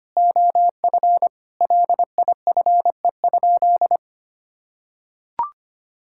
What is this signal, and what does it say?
Morse code, heard as a keyed tone, about 25 wpm, 700 Hz. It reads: OF LIFE?
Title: Sentences from Top 1000 Words - Review Part 5 - 25wpm